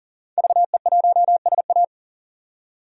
Morse code, heard as a keyed tone, about 37 wpm, 700 Hz. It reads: VE1SA
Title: DXCC-100 Call Signs - 5 Characters - 40wpm